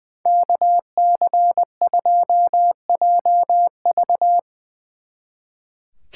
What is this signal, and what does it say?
Morse code, heard as a keyed tone, about 22 wpm, 700 Hz. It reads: KC2JV
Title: Speed Racing - USA 2x2 Call Signs 20wpm